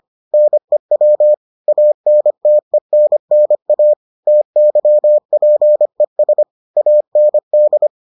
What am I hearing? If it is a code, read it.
Morse code, heard as a keyed tone, about 25 wpm, 600 Hz. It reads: NEW ANTENNA TYPES AND